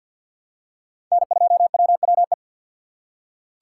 Morse code, heard as a keyed tone, about 47 wpm, 690 Hz. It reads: N2WWE
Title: USA Call Signs - 5 Characters - 3x Character Spacing - 50wpm